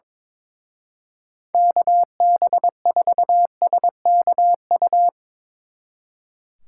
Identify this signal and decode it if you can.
Morse code, heard as a keyed tone, about 24 wpm, 700 Hz. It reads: KB4SKU
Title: Speed Racing - USA 2x3 Call Signs 22wpm